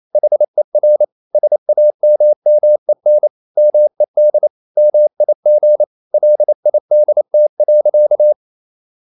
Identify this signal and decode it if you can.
Morse code, heard as a keyed tone, about 28 wpm, 600 Hz. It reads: HER SAMMEN MED MIG LIDT.
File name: Ferie QRQ 28wpm